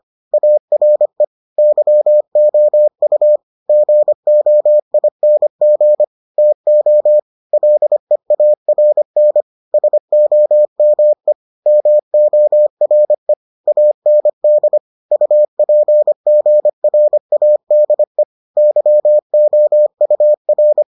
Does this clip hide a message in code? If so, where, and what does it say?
Yes: Morse code from 0.3 to 20.8 s, heard as a keyed tone, about 25 wpm, 600 Hz. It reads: ARE YOU GOING TO LEARN SOME MORE AND UPGRADE YOUR